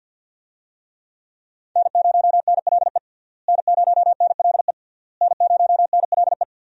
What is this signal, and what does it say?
Morse code, heard as a keyed tone, about 50 wpm, 700 Hz. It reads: N0NLE N0NLE N0NLE
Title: USA Call Signs - 5 Characters - Repeated 3x - 50wpm